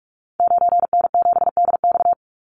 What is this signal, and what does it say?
Morse code, heard as a keyed tone, about 44 wpm, 700 Hz. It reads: 9N7DX
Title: Call Signs Repeated 3x - 45wpm